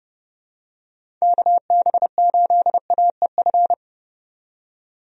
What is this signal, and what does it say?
Morse code, heard as a keyed tone, about 30 wpm, 700 Hz. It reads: KB8AEF
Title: USA Call Signs - 6 Characters - 30wpm